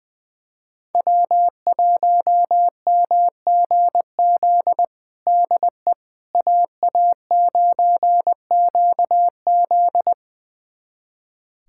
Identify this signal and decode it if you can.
Morse code, heard as a keyed tone, about 21 wpm, 700 Hz. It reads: W1MGZ DE AA9QZ
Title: USA Callsign DE Callsign - 5 Characters - 20wpm